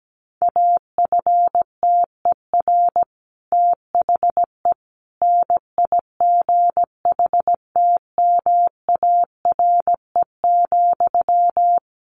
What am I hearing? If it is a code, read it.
Morse code, heard as a keyed tone, about 17 wpm, 700 Hz. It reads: AFTER THE NIGHTMARE,
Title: A Sound Of Thunder 17wpm